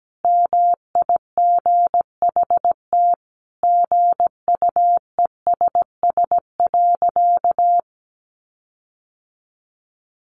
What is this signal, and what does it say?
Morse code, heard as a keyed tone, about 17 wpm, 700 Hz. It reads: MIGHT GUESS.
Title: Sentences from Top 1500 Words 17wpm